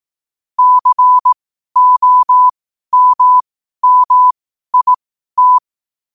Morse code, heard as a keyed tone, about 18 wpm, spaced as for 12 wpm, 1000 Hz. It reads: COMMIT
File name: qotdfiles / 12wpm